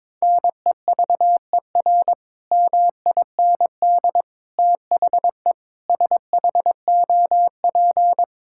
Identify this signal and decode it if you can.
Morse code, heard as a keyed tone, about 21 wpm, 700 Hz. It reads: NEVER MIND THE SHOP
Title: Sentences from Top 1600 Words 22wpm